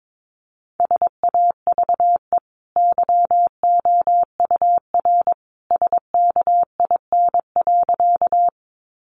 Morse code, heard as a keyed tone, about 22 wpm, 700 Hz. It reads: SAVE YOUR SKIN.
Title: Sentences from Top 1300 Words 22wpm